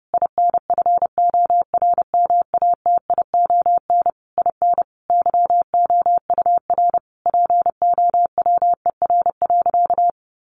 Morse code, heard as a keyed tone, about 28 wpm, 700 Hz. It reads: INFORMATION IN YOUR POWER.
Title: Sentences from Top 600 Words 30wpm